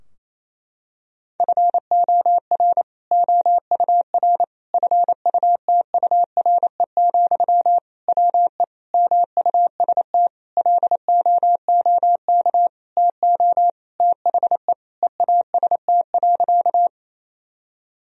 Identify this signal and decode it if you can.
Morse code, heard as a keyed tone, about 28 wpm, 700 Hz. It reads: FOR OUR FUTURE, WE MUST LOOK TO THE EAST.
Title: Sentences from Top 1000 Words - Review Part 3 - 28wpm